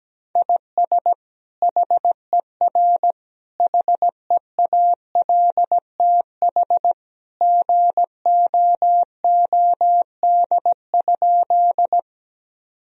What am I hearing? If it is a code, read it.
Morse code, heard as a keyed tone, about 17 wpm, 700 Hz. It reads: IS HER HEALTH GOOD?
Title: Sentences from Top 1000 Words - Review Part 3 - 17wpm